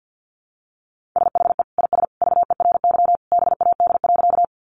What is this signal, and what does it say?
Morse code, heard as a keyed tone, about 50 wpm, 700 Hz. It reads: SHE IS VERY BAD.
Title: Sentences from Top 700 Words 50wpm